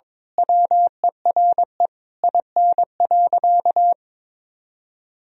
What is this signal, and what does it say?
Morse code, heard as a keyed tone, about 22 wpm, 700 Hz. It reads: WERE IN.
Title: Sentences from Top 1000 Words - Review Part 3 - 22wpm